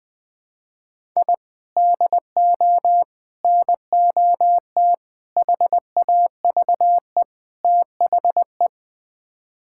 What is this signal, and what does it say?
Morse code, heard as a keyed tone, about 20 wpm, 700 Hz. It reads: I DO NOT HAVE THE
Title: Build A Phrase - 5-Word Phrases - 20wpm